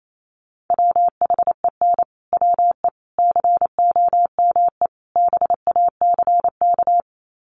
Morse code, heard as a keyed tone, about 28 wpm, 700 Hz. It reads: WHEN WE COME BACK
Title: Sets of 4 Words 28wpm